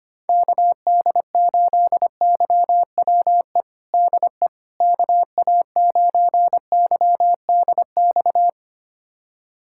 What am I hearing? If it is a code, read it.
Morse code, heard as a keyed tone, about 25 wpm, 700 Hz. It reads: KD8YWE DE KA9YDX
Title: USA Callsign DE Callsign 25wpm